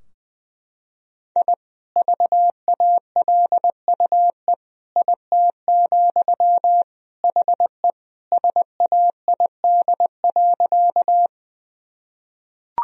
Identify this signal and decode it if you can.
Morse code, heard as a keyed tone, about 21 wpm, 700 Hz. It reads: I VALUE IT, HE SAID.
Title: Sentences from Top 1000 Words - Review Part 3 - 20wpm